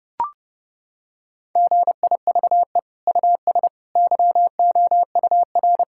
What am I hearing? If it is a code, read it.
Morse code, heard as a keyed tone, about 30 wpm, 700 Hz. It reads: GIVE US YOUR
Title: Sentences from Top 1000 Words - Review Part 5 - 30wpm